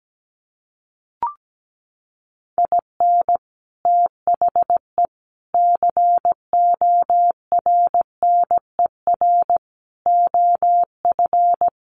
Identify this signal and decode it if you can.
Morse code, heard as a keyed tone, about 17 wpm, 700 Hz. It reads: IN THE CORNER OF
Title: Sets of 5 Words Encore 17wpm